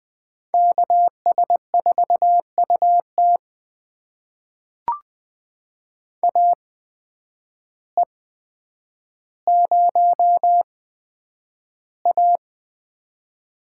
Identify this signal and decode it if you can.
Morse code, heard as a keyed tone, about 20 wpm, spaced as for 9 wpm, 700 Hz. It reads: KS4UT AE0A
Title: USA Call Signs - 5 Characters - 8x Character Spacing - 20wpm